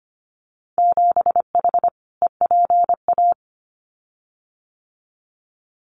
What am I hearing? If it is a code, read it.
Morse code, heard as a keyed tone, about 25 wpm, 700 Hz. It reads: 7H EPA
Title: Winter Field Day Exchanges Encore 25wpm